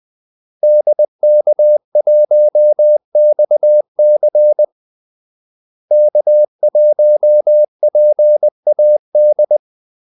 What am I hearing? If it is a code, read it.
Morse code, heard as a keyed tone, about 20 wpm, 600 Hz. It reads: DK1XC K1PAD
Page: Kaldesignaler 20 wpm